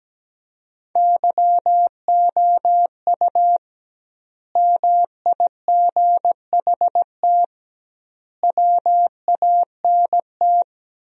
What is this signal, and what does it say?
Morse code, heard as a keyed tone, about 17 wpm, 700 Hz. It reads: YOU MIGHT WANT
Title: 2x Word-Spacing - Sentences from Top 300 Words - 17wpm